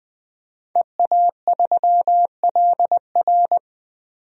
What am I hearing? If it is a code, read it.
Morse code, heard as a keyed tone, about 20 wpm, 700 Hz. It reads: EA3LR
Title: DXCC-100 Call Signs - 5 Characters - 20wpm